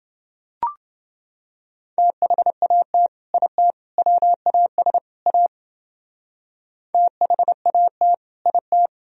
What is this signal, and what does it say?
Morse code, heard as a keyed tone, about 29 wpm, 700 Hz. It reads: THAT IT WAS A THAT IT